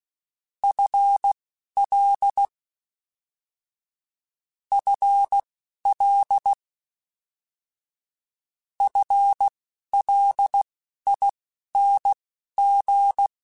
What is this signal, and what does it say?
Morse code, heard as a keyed tone, about 16 wpm, spaced as for 11 wpm, 800 Hz. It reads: FL FL FLING
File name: Combinations / Combinations-mp3s / Combinations@10